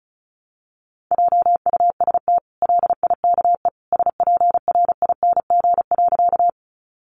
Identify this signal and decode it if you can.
Morse code, heard as a keyed tone, about 34 wpm, 700 Hz. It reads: JUST LIKE SPRING.
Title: Sentences from Top 1200 Words 35wpm